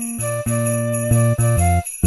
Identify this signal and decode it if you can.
There is no code.